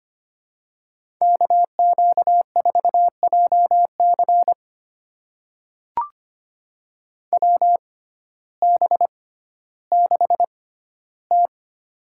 Morse code, heard as a keyed tone, about 25 wpm, spaced as for 14 wpm, 700 Hz. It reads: KQ4JC WB6T
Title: USA Call Signs - 6x Character Spacing - 25wpm